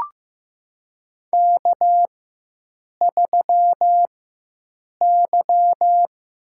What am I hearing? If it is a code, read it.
Morse code, heard as a keyed tone, about 15 wpm, spaced as for 7 wpm, 700 Hz. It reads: K3Y
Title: USA Call Signs - 4x Character Spacing - 15wpm